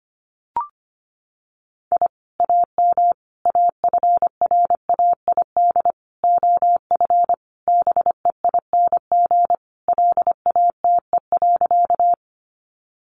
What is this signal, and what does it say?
Morse code, heard as a keyed tone, about 25 wpm, 700 Hz. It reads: I AM AFRAID OF BEING LATE.
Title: Sentences from Top 1500 Words 25wpm